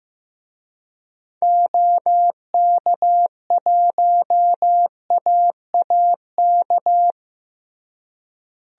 Morse code, heard as a keyed tone, about 15 wpm, 700 Hz. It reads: OK1AAK
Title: DXCC-100 Call Signs - 6 Characters - Repeated 2x - 15wpm